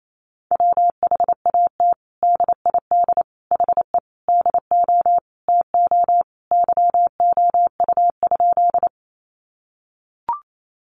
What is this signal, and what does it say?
Morse code, heard as a keyed tone, about 28 wpm, 700 Hz. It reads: WHAT DID HE DO TO YOU?
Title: Sentences from Top 100 Words 28wpm